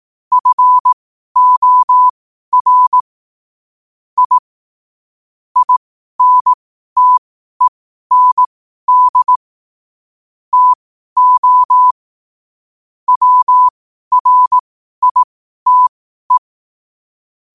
Morse code, heard as a keyed tone, about 18 wpm, spaced as for 12 wpm, 1000 Hz. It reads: FOR I INTEND TO WRITE
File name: qotdfiles / 12wpm